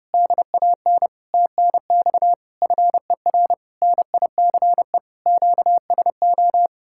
Morse code, heard as a keyed tone, about 33 wpm, 700 Hz. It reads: DAN TNX FER NICE QSO